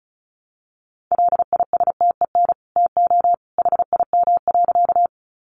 Morse code, heard as a keyed tone, about 34 wpm, 700 Hz. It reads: LISTEN TO HIM.
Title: Sentences from Top 400 Words 35wpm